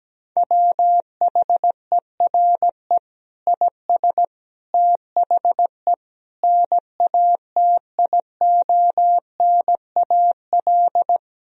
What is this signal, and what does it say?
Morse code, heard as a keyed tone, about 17 wpm, 700 Hz. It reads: WHERE IS THE NATIONAL
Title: Sentences from Top 1000 Words - Review Part 4 - 17wpm